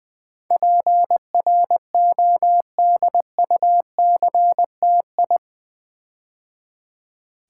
Morse code, heard as a keyed tone, about 20 wpm, 700 Hz. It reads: PRODUCTI
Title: Build A Word - 4 Letters At A Time - 12-Letter Words - 20wpm